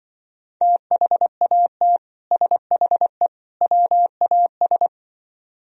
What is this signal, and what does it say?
Morse code, heard as a keyed tone, about 24 wpm, 700 Hz. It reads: THAT SHE WAS
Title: Speed-Racing - Sets of 3 Words 20wpm